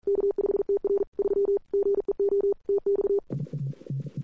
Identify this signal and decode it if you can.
Morse code, heard as a keyed tone, about 41 wpm, 410 Hz. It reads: K5TR 3 GEO TX
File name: CWSprint / Feb2007 / 2007-feb-cw-sprint-k5tr-op-n5rz -00:00